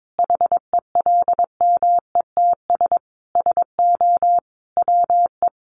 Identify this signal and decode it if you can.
Morse code, heard as a keyed tone, about 22 wpm, 700 Hz. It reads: HELMETS SO WE
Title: A Sound Of Thunder 22wpm